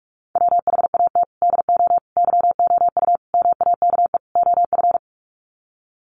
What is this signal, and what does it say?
Morse code, heard as a keyed tone, about 44 wpm, 700 Hz. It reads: WHAT DO YOU MAKE OF
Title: Sets of 5 Words 45wpm